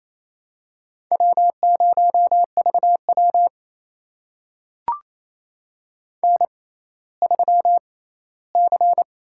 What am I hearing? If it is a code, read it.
Morse code, heard as a keyed tone, about 28 wpm, spaced as for 16 wpm, 700 Hz. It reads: W0VW N3C
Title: USA Call Signs - 4 Characters - 6x Character Spacing - 28wpm